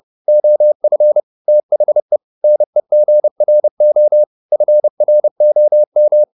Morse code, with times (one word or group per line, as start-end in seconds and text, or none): 0.3-1.2 OF
1.5-2.2 THE
2.4-4.2 NEGRO
4.5-6.2 FROM